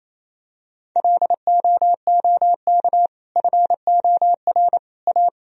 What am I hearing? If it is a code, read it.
Morse code, heard as a keyed tone, about 28 wpm, 690 Hz. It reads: LOOK FOR A
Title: Sentences from Top 1000 Words - Review Part 2 - 28wpm